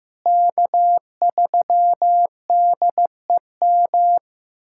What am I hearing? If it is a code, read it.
Morse code, heard as a keyed tone, about 15 wpm, 700 Hz. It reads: K3DEM
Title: USA Callsign DE Callsign - 5 Characters - 15wpm